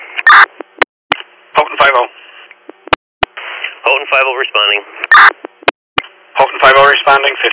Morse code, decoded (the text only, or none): none